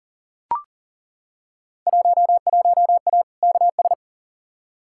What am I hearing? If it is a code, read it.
Morse code, heard as a keyed tone, about 40 wpm, 700 Hz. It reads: 11A KS
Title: Field Day Exchanges Encore - 40wpm